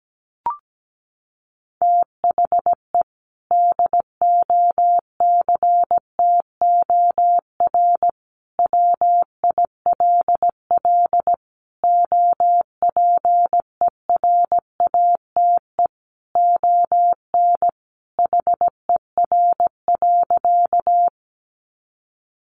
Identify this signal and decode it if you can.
Morse code, heard as a keyed tone, about 17 wpm, 700 Hz. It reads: THE DOCTOR WILL OPERATE ON HER.
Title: Sentences from Top 1500 Words 17wpm